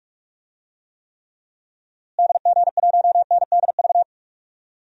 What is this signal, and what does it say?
Morse code, heard as a keyed tone, about 42 wpm, 700 Hz. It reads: DG1NDV
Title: DXCC-100 Call Signs - 6 Characters - Repeated 3x - 45wpm